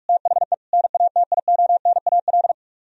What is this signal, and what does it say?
Morse code, heard as a keyed tone, about 45 wpm, 700 Hz. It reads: THE NATIONAL